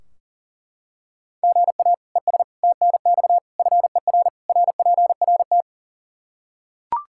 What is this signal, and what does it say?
Morse code, heard as a keyed tone, about 40 wpm, 700 Hz. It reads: GA ES TNX FER RPRT
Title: K1USN SST QSO Practice 40wpm